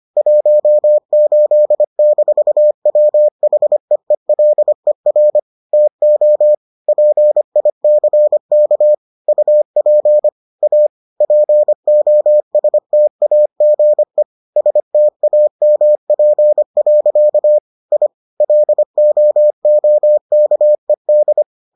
Morse code, heard as a keyed tone, about 25 wpm, 600 Hz. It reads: 18-WHEELER TO PICK UP A POSTAGE STAMP. I LOOKED